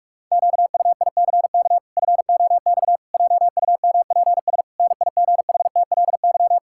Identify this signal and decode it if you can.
Morse code, heard as a keyed tone, about 46 wpm, 700 Hz. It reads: QUICK FOX JUMPS NIGHTLY
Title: Pangram Challenge v2 - 45wpm